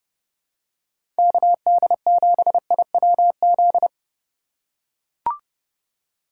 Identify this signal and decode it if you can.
Morse code, heard as a keyed tone, about 32 wpm, 700 Hz. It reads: KD7IWZ
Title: USA Call Signs - 6 Characters - 2x Character Spacing - 30wpm